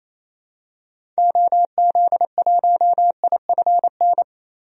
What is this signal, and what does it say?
Morse code, heard as a keyed tone, about 28 wpm, 700 Hz. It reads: OZ1IFN
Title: DXCC-100 Call Signs - 6 Characters - Repeated 3x - 28wpm